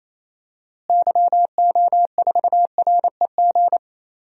Morse code, heard as a keyed tone, about 28 wpm, 700 Hz. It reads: YO4REG
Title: DXCC-100 Call Signs - 6 Characters - Repeated 3x - 28wpm